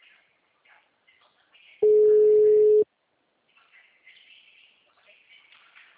Ringback tone (Europe, 425 Hz).